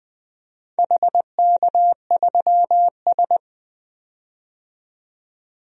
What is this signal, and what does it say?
Morse code, heard as a keyed tone, about 20 wpm, 700 Hz. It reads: HK3S
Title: DXCC-100 Call Signs - 4 Characters - 20wpm